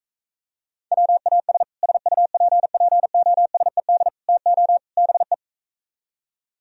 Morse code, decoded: WAS SUPPOSED TO BE

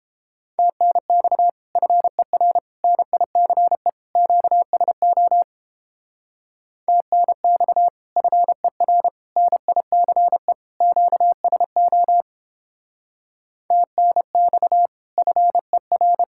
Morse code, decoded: TNX FER NICE QSO TNX FER NICE QSO TNX FER

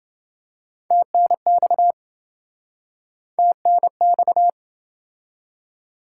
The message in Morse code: TNX TNX

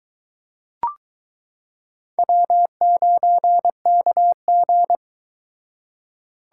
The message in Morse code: W9KG

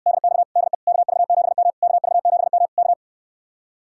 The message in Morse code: DX DE KV7M KV7M K

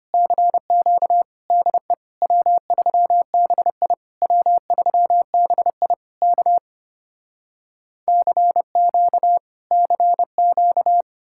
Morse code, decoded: CQ DE W3BI W3BI K CQ CQ

